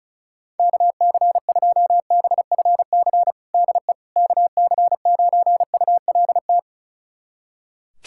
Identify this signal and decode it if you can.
Morse code, heard as a keyed tone, about 34 wpm, 700 Hz. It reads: KC2BFC DE KC9ULT
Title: USA Callsign DE Callsign - 6 Characters - 35wpm